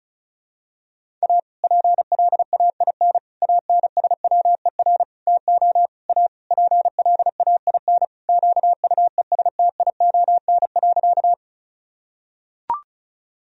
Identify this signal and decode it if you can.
Morse code, heard as a keyed tone, about 34 wpm, 700 Hz. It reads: A PLAIN ANSWER TO A PLAIN QUESTION.
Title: Sentences from Top 1000 Words - Review Part 2 - 35wpm